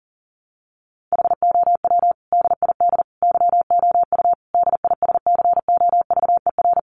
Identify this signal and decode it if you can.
Morse code, heard as a keyed tone, about 37 wpm, 700 Hz. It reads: HOW DID YOU DISCOVER
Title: Sentences from Top 1100 Words 40wpm